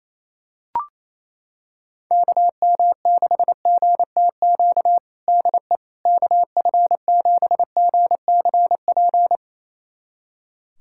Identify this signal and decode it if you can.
Morse code, heard as a keyed tone, about 28 wpm, 700 Hz. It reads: KM6GTQ DE KF7GCP
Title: USA Callsign DE Callsign - 6 Characters - 28wpm